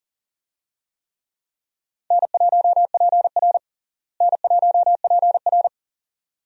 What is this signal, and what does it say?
Morse code, heard as a keyed tone, about 40 wpm, 700 Hz. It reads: N1PR N1PR
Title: USA Call Signs - 4 Characters - Repeated 2x - 40wpm